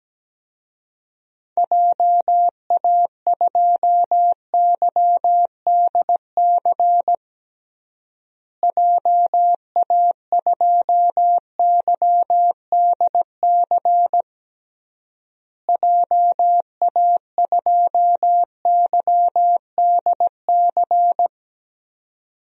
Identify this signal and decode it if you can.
Morse code, heard as a keyed tone, about 17 wpm, 700 Hz. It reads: JA2YDC JA2YDC JA2YDC
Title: DXCC-100 Call Signs - 6 Characters - Repeated 3x - 17wpm